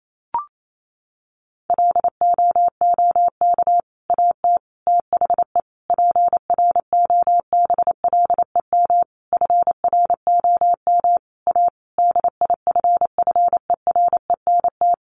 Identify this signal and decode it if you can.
Morse code, heard as a keyed tone, about 28 wpm, 700 Hz. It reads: LOOK AT THE PROBLEM FROM A DIFFERENT
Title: Sentences from Top 1500 Words 28wpm